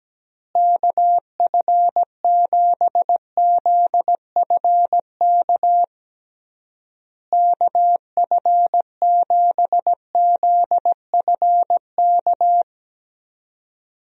Morse code, heard as a keyed tone, about 17 wpm, 700 Hz. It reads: KF7ZFK KF7ZFK
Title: USA Call Signs - 6 Characters - Repeated 3x - 17wpm